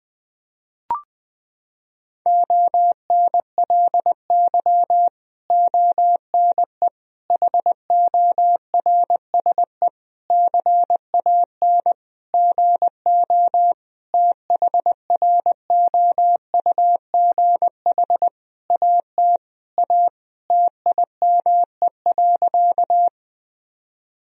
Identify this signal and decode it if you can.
Morse code, heard as a keyed tone, about 21 wpm, 700 Hz. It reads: ONLY ONE HORSE CAN GO THROUGH AT A TIME.
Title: Sentences from Top 1000 Words - Review Part 1 - 20wpm